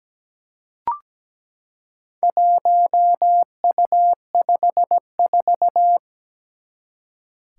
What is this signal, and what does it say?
Morse code, heard as a keyed tone, about 17 wpm, 700 Hz. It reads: 1U54
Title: Four Challenging Letter-Number Combinations 17wpm